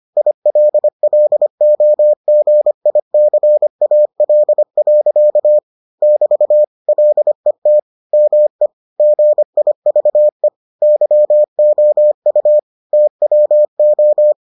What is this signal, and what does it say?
Morse code, heard as a keyed tone, about 25 wpm, 600 Hz. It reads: ILLOGICAL. <BT> LET ME GIVE YOU TWO